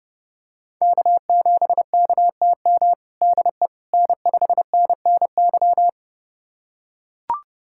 Morse code, heard as a keyed tone, about 30 wpm, 700 Hz. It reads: K7KTM DE N5NNY
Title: USA Callsign DE Callsign 30wpm